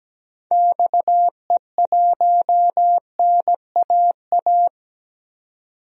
Morse code, heard as a keyed tone, about 17 wpm, 700 Hz. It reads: XE1NAA
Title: DXCC-100 Call Signs - 6 Characters - 17wpm